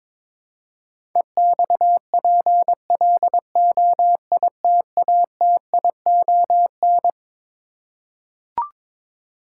Morse code, heard as a keyed tone, about 21 wpm, 700 Hz. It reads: EXPLOITATION
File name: practice / 22wpm / Twelve Letter Words - 22wpm